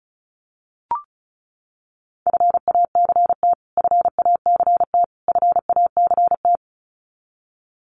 Morse code, heard as a keyed tone, about 34 wpm, 700 Hz. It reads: FACT FACT FACT
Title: Top 500 Words Repeated 35wpm